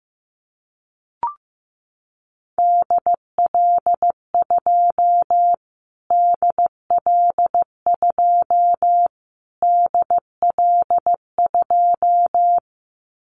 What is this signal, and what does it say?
Morse code, heard as a keyed tone, about 15 wpm, 700 Hz. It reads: DL2 DL2 DL2
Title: Callsign Prefixes for Top 25 DXCC - Repeated 3x - 15wpm